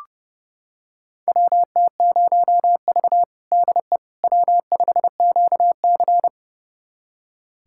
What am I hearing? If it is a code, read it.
Morse code, heard as a keyed tone, about 30 wpm, 700 Hz. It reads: WT0V DE W5QC
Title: USA Callsign DE Callsign - 4 Characters - 30wpm